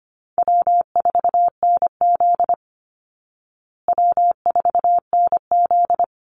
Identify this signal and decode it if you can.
Morse code, heard as a keyed tone, about 25 wpm, 700 Hz. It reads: W4NZ W4NZ
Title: Easy Call Signs Repeated 3x - 25wpm